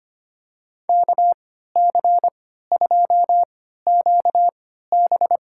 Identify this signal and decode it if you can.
Morse code, heard as a keyed tone, about 25 wpm, spaced as for 14 wpm, 700 Hz. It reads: KC2QB
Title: USA Call Signs - 3x Character Spacing - 25wpm